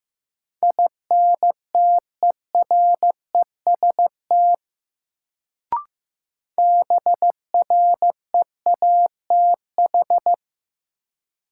Morse code, heard as a keyed tone, about 16 wpm, 700 Hz. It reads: INTEREST BREATH
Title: Sets of 5 Words Encore - New Word Drill - 15wpm